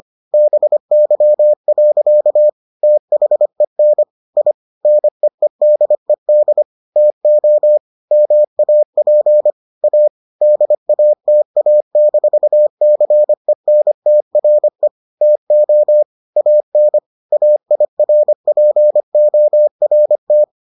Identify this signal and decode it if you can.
Morse code, heard as a keyed tone, about 25 wpm, 600 Hz. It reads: BY. THEN I NEEDED TO MAP A DATA-CENTRE TO AN AIRPORT